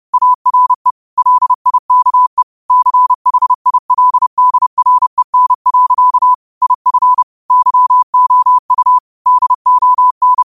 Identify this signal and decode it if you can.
Morse code, heard as a keyed tone, about 30 wpm, 1000 Hz. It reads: ARE LIKE CHILDREN. IF YOU DON
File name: qotdfiles / 30wpm